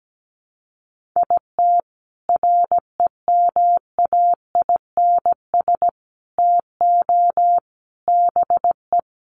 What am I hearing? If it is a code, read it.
Morse code, heard as a keyed tone, about 17 wpm, 700 Hz. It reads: IT REMAINS TO BE
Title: Sets of 5 Words Encore 17wpm